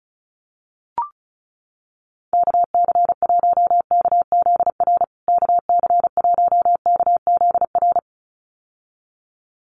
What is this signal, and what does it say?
Morse code, heard as a keyed tone, about 34 wpm, 700 Hz. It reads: KC1KZR KC1KZR
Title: USA Call Signs Repeated 2x 35wpm